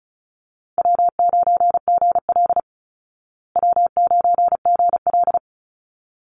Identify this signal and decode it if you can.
Morse code, heard as a keyed tone, about 34 wpm, 700 Hz. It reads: W9GL W9GL
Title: Easy Call Signs Repeated 2x - 35wpm